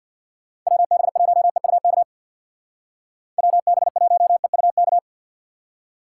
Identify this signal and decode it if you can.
Morse code, heard as a keyed tone, about 63 wpm, 700 Hz. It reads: WB1EUK WB1EUK